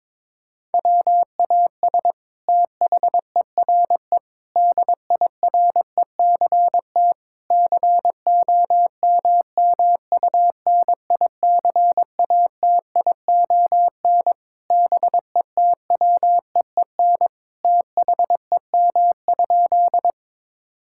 Morse code, heard as a keyed tone, about 22 wpm, 700 Hz. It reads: WAS THERE DIRECT COMMUNICATION BETWEEN THEM?
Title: Sentences from Top 1000 Words - Review Part 6 - 22wpm